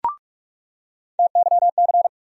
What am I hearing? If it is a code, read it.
Morse code, heard as a keyed tone, about 46 wpm, 700 Hz. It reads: TYC
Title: USA Call Sign Postfixes - 3 Letters 45wpm